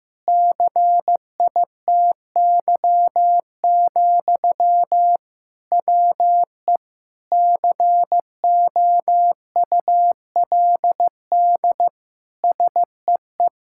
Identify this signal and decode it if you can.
Morse code, heard as a keyed tone, about 15 wpm, 700 Hz. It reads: CITY, WE COULD SEE
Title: Sentences from Top 1000 Words - Review Part 5 - 15wpm